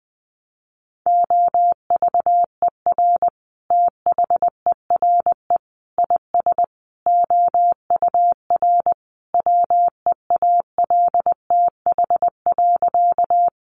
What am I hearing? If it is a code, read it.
Morse code, heard as a keyed tone, about 20 wpm, 700 Hz. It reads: OVER THERE IS OUR WEALTH.
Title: Sentences from Top 1200 Words 20wpm